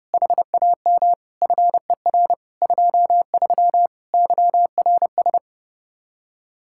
Morse code, heard as a keyed tone, about 29 wpm, 700 Hz. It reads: HAM FER 23 YRS
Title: Practice QSO - Exchange 3 - More Info - 30wpm